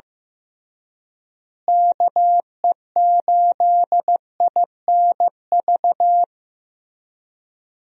Morse code, heard as a keyed tone, about 16 wpm, 700 Hz. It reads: KE8INV
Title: USA Call Signs - 6 Characters - Repeated 3x - 15wpm